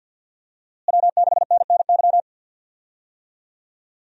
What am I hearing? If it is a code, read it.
Morse code, heard as a keyed tone, about 52 wpm, 700 Hz. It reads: W6NNY
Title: USA Call Signs - 5 Characters - Repeated 3x - 50wpm